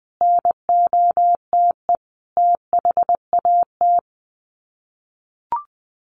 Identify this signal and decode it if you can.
Morse code, heard as a keyed tone, about 20 wpm, 700 Hz. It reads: NOTE THAT
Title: Sets of 5 Words Encore2 20wpm